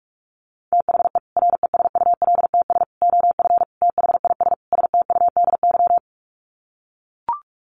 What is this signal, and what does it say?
Morse code, heard as a keyed tone, about 44 wpm, 700 Hz. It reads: THE RESULTS OF THIS STUDY
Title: Sets of 5 Words 45wpm